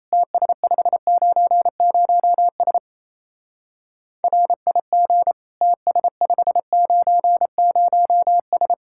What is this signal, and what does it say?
Morse code, heard as a keyed tone, about 34 wpm, 700 Hz. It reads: TS590S RIG TS590S
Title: Speed-Racing Common QSO Phrases - 28wpm